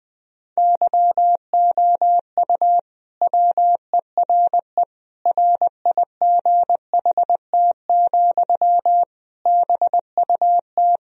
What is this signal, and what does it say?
Morse code, heard as a keyed tone, about 20 wpm, 700 Hz. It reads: YOU WERE RIGHT, BUT
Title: Sentences from Top 1000 Words - Review Part 2 - 20wpm